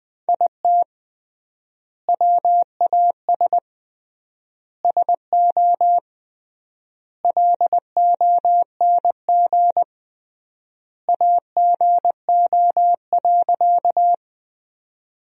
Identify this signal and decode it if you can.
Morse code, heard as a keyed tone, about 20 wpm, 700 Hz. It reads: IT WAS SO LONG AGO.
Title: 3x Word-Spacing - Sentences from Top 500 Words - 20wpm